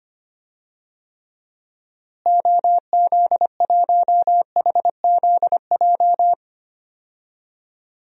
Morse code, heard as a keyed tone, about 25 wpm, 700 Hz. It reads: OZ1HZJ
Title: DXCC-100 Call Signs - 6 Characters - Repeated 3x - 25wpm